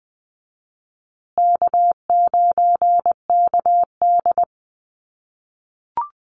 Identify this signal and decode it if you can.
Morse code, heard as a keyed tone, about 20 wpm, 700 Hz. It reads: K9KD
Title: USA Call Signs Repeated 3x 20wpm